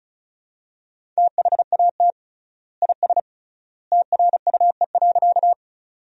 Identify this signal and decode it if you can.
Morse code, heard as a keyed tone, about 34 wpm, 700 Hz. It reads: THAT IS TRUE.
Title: 3x Word-Spacing - Sentences from Top 400 Words - 35wpm